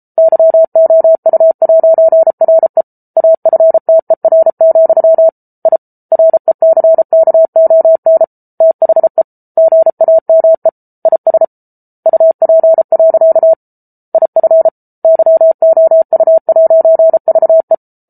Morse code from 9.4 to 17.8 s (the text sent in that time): GAME IS UP. IF YOU'VE